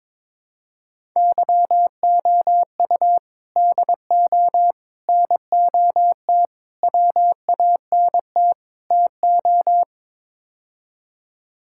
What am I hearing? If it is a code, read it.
Morse code, heard as a keyed tone, about 22 wpm, 700 Hz. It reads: YOU DO NOT WANT TO
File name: practice / 22wpm / Build A Phrase - 5-Word Phrases - 22wpm